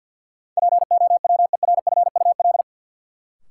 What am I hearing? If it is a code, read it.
Morse code, heard as a keyed tone, about 49 wpm, 700 Hz. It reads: POWERFUL
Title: Sentences from Top 1300 Words - New Word Drill Alt - 50wpm